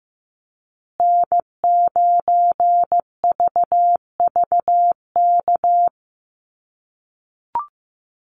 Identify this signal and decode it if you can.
Morse code, heard as a keyed tone, about 15 wpm, 700 Hz. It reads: N9VVK
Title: USA Call Signs Repeated 2x 15wpm